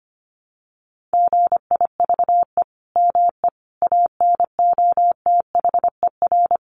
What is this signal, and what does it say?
Morse code, heard as a keyed tone, about 25 wpm, 700 Hz. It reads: GIVE ME ANOTHER
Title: Sentences from Top 1400 Words 25wpm